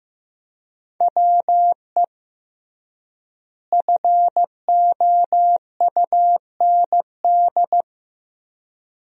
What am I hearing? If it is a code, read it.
Morse code, heard as a keyed tone, about 15 wpm, 700 Hz. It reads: WE FOUND
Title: 3x Word-Spacing - Sentences from Top 300 Words - 15wpm